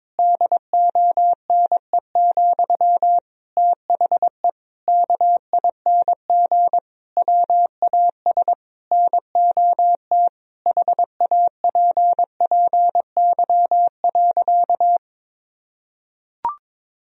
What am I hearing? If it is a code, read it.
Morse code, heard as a keyed tone, about 22 wpm, 700 Hz. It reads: DONE, THE KING WAS NOT HAPPY.
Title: Sentences from Top 1000 Words - Review Part 3 - 22wpm